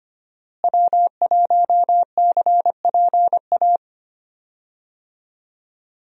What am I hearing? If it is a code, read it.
Morse code, heard as a keyed tone, about 25 wpm, 700 Hz. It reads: W1CPA
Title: USA Call Signs - 5 Characters - Repeated 2x - 25wpm